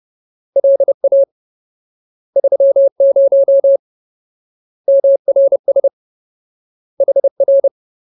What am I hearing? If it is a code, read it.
Morse code, heard as a keyed tone, about 30 wpm, 550 Hz. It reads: LA 30 MRS HR
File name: Ultimate Word List / 30WPM Ultimate